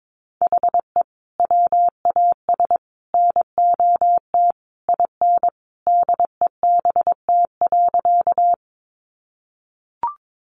Morse code, heard as a keyed tone, about 22 wpm, 700 Hz. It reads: HE WAS NOT IN DEBT.
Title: Sentences from Top 1300 Words 22wpm